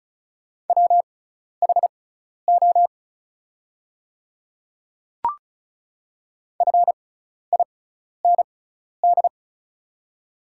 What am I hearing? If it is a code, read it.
Morse code, heard as a keyed tone, about 36 wpm, spaced as for 12 wpm, 690 Hz. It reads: WHO FIND
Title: 6x Character Spacing - Top 300 Words - 35wpm